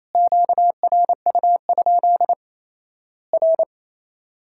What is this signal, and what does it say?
Morse code, heard as a keyed tone, about 28 wpm, 700 Hz. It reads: QRU? R
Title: Worked All Europe DX Contest - QTC Exchange - 28wpm